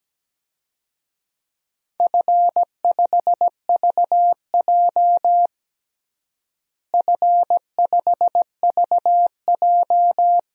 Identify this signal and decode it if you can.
Morse code, heard as a keyed tone, about 17 wpm, 700 Hz. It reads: F5VJ F5VJ
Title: DXCC-100 Call Signs - 4 Characters - Repeated 2x - 17wpm